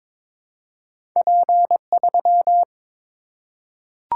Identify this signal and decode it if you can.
Morse code, heard as a keyed tone, about 22 wpm, 700 Hz. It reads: P3
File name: practice / 22wpm / Two Letter-Number Combinations - 22wpm